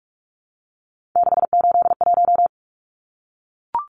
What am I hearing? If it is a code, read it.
Morse code, heard as a keyed tone, about 44 wpm, 700 Hz. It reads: 681
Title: Numbers - 3-Digit 45wpm